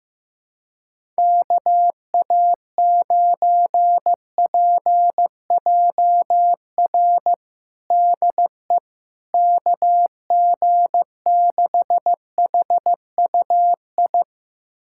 Morse code, heard as a keyed tone, about 15 wpm, 700 Hz. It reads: KA9PJR DE KG6HUI
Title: USA Callsign DE Callsign 15wpm